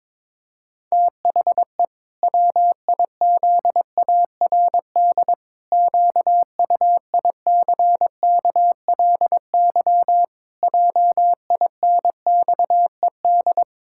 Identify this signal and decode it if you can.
Morse code, heard as a keyed tone, about 22 wpm, 700 Hz. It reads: THE WIZARD QUICKLY JINXED
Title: Pangram Challenge v2 - 22wpm